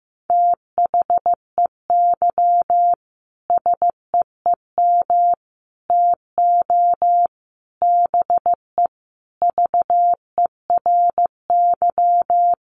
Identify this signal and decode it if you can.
Morse code, heard as a keyed tone, about 15 wpm, 700 Hz. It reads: THEY SEEM TO BE VERY
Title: Sentences from Top 1100 Words 15wpm